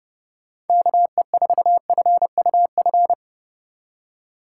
Morse code, heard as a keyed tone, about 35 wpm, 700 Hz. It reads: KE4FUF
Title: Speed Racing - USA Call Signs 30wpm